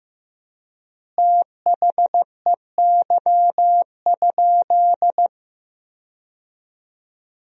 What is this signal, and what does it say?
Morse code, heard as a keyed tone, about 16 wpm, 700 Hz. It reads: THEY?
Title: 3x Word-Spacing - Sentences from Top 300 Words - 15wpm